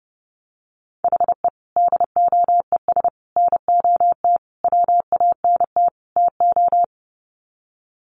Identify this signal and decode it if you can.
Morse code, heard as a keyed tone, about 30 wpm, 700 Hz. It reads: HE DOES NOT WANT TO
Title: Sets of 5 Words 30wpm